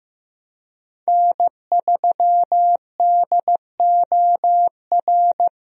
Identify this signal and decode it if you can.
Morse code, heard as a keyed tone, about 15 wpm, 700 Hz. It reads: N3DOR
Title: USA Call Signs - 5 Characters - 15wpm